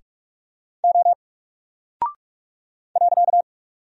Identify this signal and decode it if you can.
Morse code, heard as a keyed tone, about 44 wpm, 700 Hz. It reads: O .
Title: Single Character (Letter:Number:Punctuation:Prosign) 45wpm